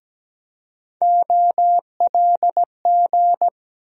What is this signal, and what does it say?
Morse code, heard as a keyed tone, about 17 wpm, 700 Hz. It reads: OLG
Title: USA Call Sign Postfixes - 3 Letters 17wpm